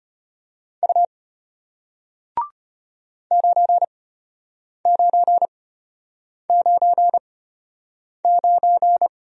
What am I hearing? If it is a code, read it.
Morse code, heard as a keyed tone, about 33 wpm, spaced as for 8 wpm, 700 Hz. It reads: U 9999